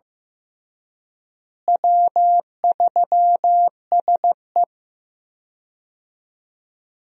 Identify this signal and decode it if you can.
Morse code, heard as a keyed tone, about 15 wpm, 700 Hz. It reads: W3SE
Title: USA Call Signs - 4 Characters - Repeated 2x - 15wpm